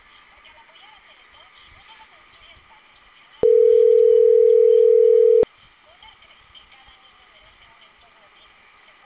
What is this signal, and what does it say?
Ringback tone (North America), 440 + 480 Hz, with speech.
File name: CENTRAL CAEA / out agcaea02 - .11831